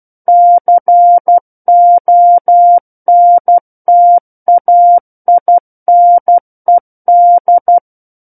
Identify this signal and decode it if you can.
Morse code, heard as a keyed tone, about 12 wpm, 700 Hz. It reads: CONTAINED
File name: Book One / 03-Chapter 3 - 12 WPM